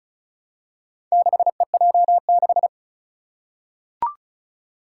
Morse code, heard as a keyed tone, about 36 wpm, 700 Hz. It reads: 6EJ6
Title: Four Challenging Letter-Number Combinations 35wpm